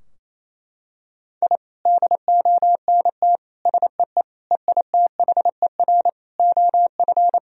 Morse code, heard as a keyed tone, about 27 wpm, 700 Hz. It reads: I DONT SEE EITHER OF
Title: Sentences from Top 1000 Words - Review Part 4 - 28wpm